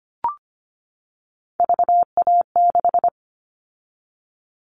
Morse code, heard as a keyed tone, about 25 wpm, 700 Hz. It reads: VA6
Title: Callsign Prefixes for Top 100 DXCC - 25wpm